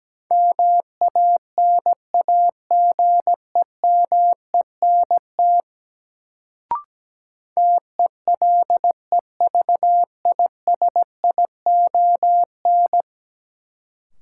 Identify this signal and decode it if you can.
Morse code, heard as a keyed tone, about 17 wpm, 700 Hz. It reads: MANAGEMENT TELEVISION
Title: Sentences from Top 1000 Words - Review Part 4 - New Word Drill - 17wpm